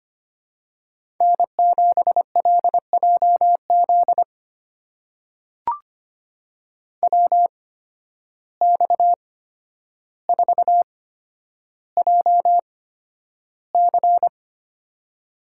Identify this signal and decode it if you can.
Morse code, heard as a keyed tone, about 25 wpm, spaced as for 11 wpm, 700 Hz. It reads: N7LJZ WX4JC
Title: USA Call Signs - 5 Characters - 8x Character Spacing - 25wpm